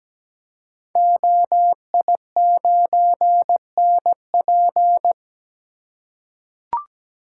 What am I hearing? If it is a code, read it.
Morse code, heard as a keyed tone, about 17 wpm, 700 Hz. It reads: OI9NP
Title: Code Groups - 5 Letter:Numbers - 17wpm